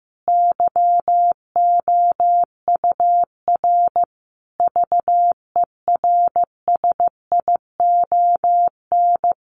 Morse code, heard as a keyed tone, about 15 wpm, 700 Hz. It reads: YOUR VERSION
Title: Sentences from Top 1400 Words 15wpm